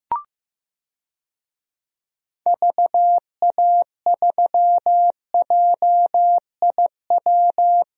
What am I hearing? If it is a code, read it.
Morse code, heard as a keyed tone, about 15 wpm, 700 Hz. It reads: VA3JIW
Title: DXCC-100 Call Signs - 6 Characters - 15wpm